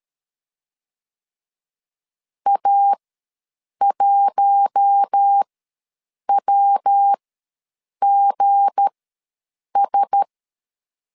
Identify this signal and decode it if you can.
Morse code, heard as a keyed tone, about 13 wpm, spaced as for 7 wpm, 800 Hz. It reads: A1WGS
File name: LEZIONI / 9 2867 ws 12wpm 3431